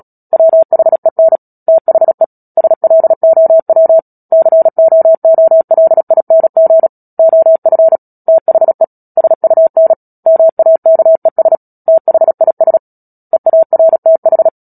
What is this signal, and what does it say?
Morse code, heard as a keyed tone, about 37 wpm, 650 Hz. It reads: WHEN THE SLOW COOLING OF THE SUN MAKES THIS EARTH
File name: Book Two / 10-Chapter 10 - Epilogue - 36 WPM